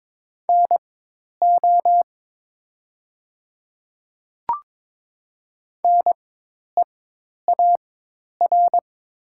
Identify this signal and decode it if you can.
Morse code, heard as a keyed tone, about 22 wpm, spaced as for 10 wpm, 700 Hz. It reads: NO NEAR
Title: 4x Character Spacing - Top 300 Words - 22wpm